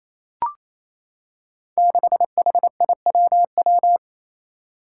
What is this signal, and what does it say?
Morse code, heard as a keyed tone, about 28 wpm, 700 Hz. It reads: 6HIWW
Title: Five Challenging Letter-Number Combinations 28wpm